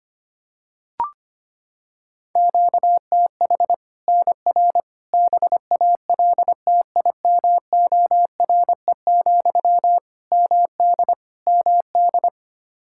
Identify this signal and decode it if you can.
Morse code, heard as a keyed tone, about 25 wpm, 700 Hz. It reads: QTH NR BALTIMORE, MD MD
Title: Practice QSO - Exchange 1 - Beginning - 25wpm